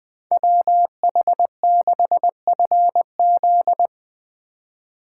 Morse code, heard as a keyed tone, about 20 wpm, 700 Hz. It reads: WH6FZ
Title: Speed Racing - USA 2x2 Call Signs 17wpm